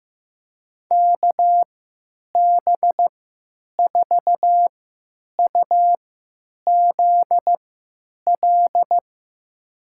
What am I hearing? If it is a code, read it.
Morse code, heard as a keyed tone, about 16 wpm, spaced as for 9 wpm, 700 Hz. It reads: KB4UZL
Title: USA Call Signs - 3x Character Spacing - 15wpm